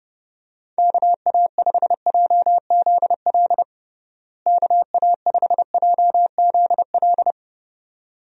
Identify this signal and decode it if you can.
Morse code, heard as a keyed tone, about 30 wpm, 700 Hz. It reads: KA5JZL KA5JZL
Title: USA Call Signs - 6 Characters - Repeated 2x - 30wpm